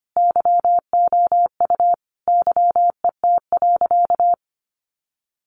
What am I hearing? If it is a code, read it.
Morse code, heard as a keyed tone, about 25 wpm, 700 Hz. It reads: YOU YET.
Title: A Sound Of Thunder 25wpm